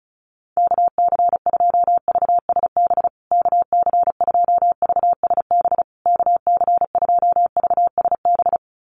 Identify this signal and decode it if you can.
Morse code, heard as a keyed tone, about 34 wpm, 700 Hz. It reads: KC2VSB KC2VSB KC2VSB
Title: USA Call Signs Repeated 3x 35wpm